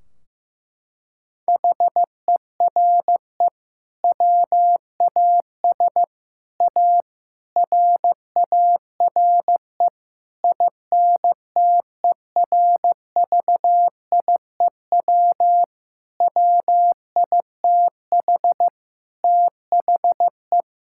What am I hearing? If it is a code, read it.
Morse code, heard as a keyed tone, about 15 wpm, 700 Hz. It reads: HERE WAS A RARE INTERVIEW WITH THE
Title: Sentences from Top 1600 Words 15wpm